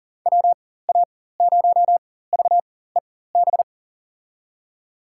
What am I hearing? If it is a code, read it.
Morse code, heard as a keyed tone, about 40 wpm, spaced as for 19 wpm, 700 Hz. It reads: WA0VEB
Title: USA Call Signs - 6 Characters - 4x Character Spacing - 40wpm